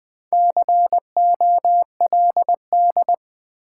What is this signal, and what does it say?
Morse code, heard as a keyed tone, about 21 wpm, 700 Hz. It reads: COLD
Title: Sets of 5 Words - New Word Drill - 20wpm